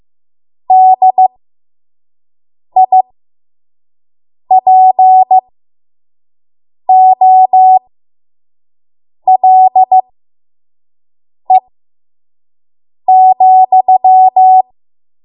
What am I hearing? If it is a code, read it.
Morse code, heard as a keyed tone, about 15 wpm, spaced as for 5 wpm, 750 Hz. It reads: DIPOLE,
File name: Morse / 05WPM